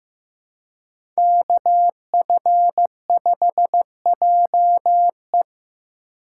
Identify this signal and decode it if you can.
Morse code, heard as a keyed tone, about 15 wpm, 700 Hz. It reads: KF5JE
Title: USA Call Signs - 2x Character Spacing - 15wpm